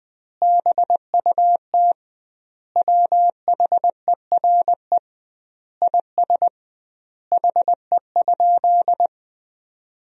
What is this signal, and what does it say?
Morse code, heard as a keyed tone, about 20 wpm, 700 Hz. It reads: BUT WHERE IS HE?
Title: 2x Word-Spacing - Sentences from Top 200 Words - 20wpm